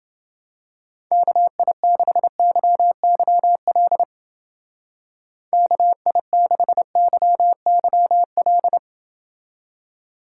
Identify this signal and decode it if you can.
Morse code, heard as a keyed tone, about 29 wpm, 700 Hz. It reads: KI6YYL KI6YYL